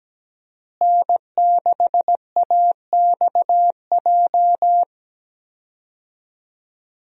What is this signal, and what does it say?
Morse code, heard as a keyed tone, about 17 wpm, 700 Hz. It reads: N6AXJ
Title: USA Call Signs - 5 Characters - Repeated 2x - 17wpm